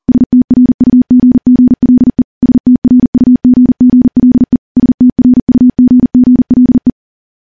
Morse code, heard as a keyed tone, about 40 wpm, 250 Hz. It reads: STRUGGLE STRUGGLE STRUGGLE
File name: morse / morse-training-40WPM